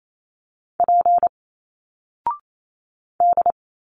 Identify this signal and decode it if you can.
Morse code, heard as a keyed tone, about 28 wpm, 700 Hz. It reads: P D
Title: Single Letters 28wpm